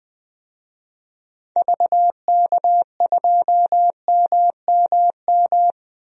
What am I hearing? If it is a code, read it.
Morse code, heard as a keyed tone, about 21 wpm, 700 Hz. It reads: VK2MMM
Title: DXCC-100 Call Signs - 6 Characters - 20wpm